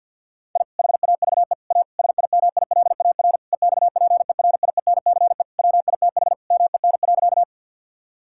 Morse code, heard as a keyed tone, about 50 wpm, 700 Hz. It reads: I HAVE A SIMILAR EXPERIENCE WITH MEN.
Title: Sentences from Top 1000 Words - Review Part 5 - 50wpm